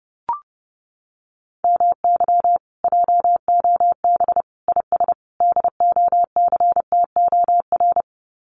Morse code, heard as a keyed tone, about 30 wpm, 700 Hz. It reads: MY JOB IS DOCTOR
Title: Common QSO Phrases Encore 30wpm